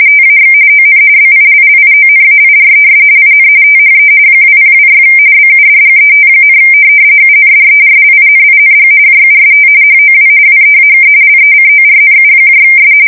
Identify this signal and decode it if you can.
RTTY, about 45.45 baud. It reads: TAX ON BILLIONAIRES IN THE 2026 BUDGET, UNDERLINING JUST HOW WEAK LECORNUS
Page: 170 Shift 2125/2295